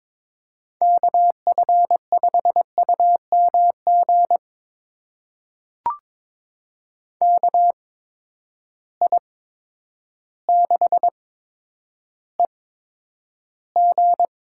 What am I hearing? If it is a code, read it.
Morse code, heard as a keyed tone, about 22 wpm, spaced as for 10 wpm, 700 Hz. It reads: KF5UMG KI6EG
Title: USA Call Signs - 8x Character Spacing - 22wpm